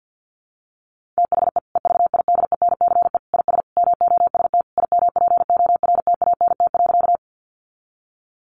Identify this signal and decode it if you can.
Morse code, heard as a keyed tone, about 50 wpm, 700 Hz. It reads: THE EVIDENCE IS MOST IMPORTANT.
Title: Sentences from Top 1300 Words 50wpm